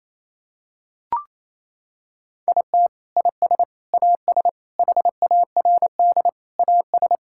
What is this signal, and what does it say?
Morse code, heard as a keyed tone, about 28 wpm, 700 Hz. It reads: IT IS AS HARD AS
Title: Sentences from Top 1600 Words 28wpm